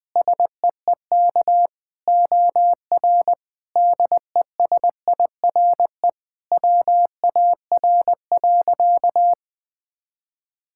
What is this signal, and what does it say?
Morse code, heard as a keyed tone, about 21 wpm, 700 Hz. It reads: SEEK OR DESIRE WAR.
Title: Sentences from Top 1600 Words 20wpm